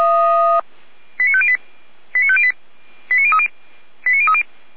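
Five-tone selective call (ZVEI), heard as quick runs of five tones, tones of about 70 ms. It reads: 89508 89508 89030 89030
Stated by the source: ZVEI five-tone call 89099 received by SDR.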